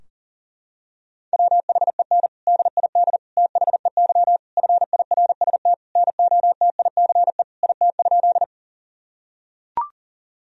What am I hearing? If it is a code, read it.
Morse code, heard as a keyed tone, about 37 wpm, 700 Hz. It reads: WHEN DID THEY FIRST NOTICE IT?
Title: Sentences from Top 1000 Words - Review Part 2 - 40wpm